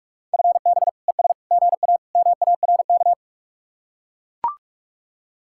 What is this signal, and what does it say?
Morse code, heard as a keyed tone, about 50 wpm, 700 Hz. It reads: FB ES GA MARK